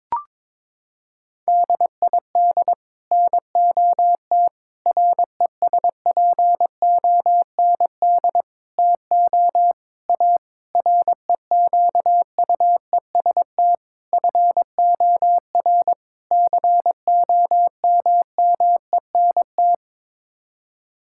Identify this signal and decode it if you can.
Morse code, heard as a keyed tone, about 21 wpm, 700 Hz. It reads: DID NOT RESPOND TO A REQUEST FOR COMMENT
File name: practice / 22wpm / Sets of 8 Words - 22wpm